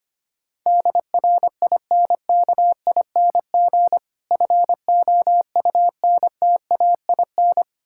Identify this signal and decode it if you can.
Morse code, heard as a keyed tone, about 26 wpm, 700 Hz. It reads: DRINKING FOUNTAIN
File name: practice / 25wpm / Difficult Words - Speed-Racing with Hints - 25wpm